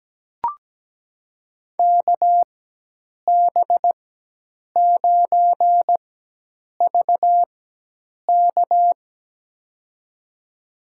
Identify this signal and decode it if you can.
Morse code, heard as a keyed tone, about 17 wpm, spaced as for 8 wpm, 700 Hz. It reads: KB9VK
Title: USA Call Signs - 5 Characters - 4x Character Spacing - 17wpm